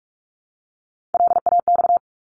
Morse code, heard as a keyed tone, about 44 wpm, 700 Hz. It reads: LAX
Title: ARRL Sections 45wpm